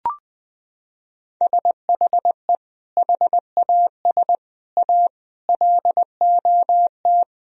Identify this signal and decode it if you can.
Morse code, heard as a keyed tone, about 19 wpm, 700 Hz. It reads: SHE HAS A LOT
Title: Sentences from Top 1000 Words - Review Part 3 - 20wpm